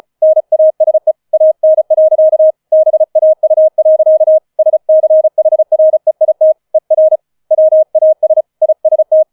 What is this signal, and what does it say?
Morse code, heard as a keyed tone, about 34 wpm, 630 Hz. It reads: NASE AN. BAU. SCHREIT ER WAS IST
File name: 005 Fünfter Streich 35wpm